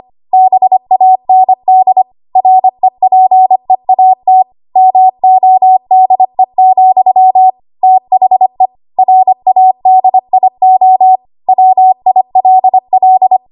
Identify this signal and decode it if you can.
Morse code, heard as a keyed tone, about 25 wpm, 750 Hz. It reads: BAND REPEAT MODE, THE RADIO WILL
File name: Archive / 25 WPM / 25WPM